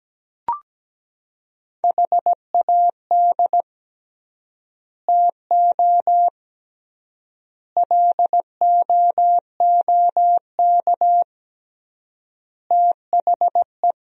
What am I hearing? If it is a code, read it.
Morse code, heard as a keyed tone, about 17 wpm, 700 Hz. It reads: HAD TO LOOK THE
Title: 3x Word-Spacing - Sentences from Top 100 Words - 17wpm